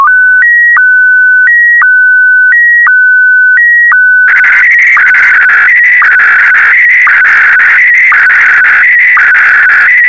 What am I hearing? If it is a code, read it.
SSTV (slow-scan television), Scottie DX.